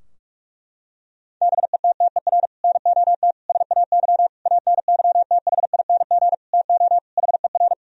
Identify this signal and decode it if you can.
Morse code, heard as a keyed tone, about 41 wpm, 700 Hz. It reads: BETTER NOT SAY ANYTHING TO HER
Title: Sentences from Top 1000 Words - Review Part 4 - 45wpm